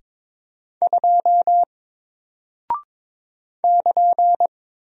Morse code, heard as a keyed tone, about 22 wpm, 700 Hz. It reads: 2 <KN>
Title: Single Character (Letter:Number:Punctuation:Prosign) Alt 22wpm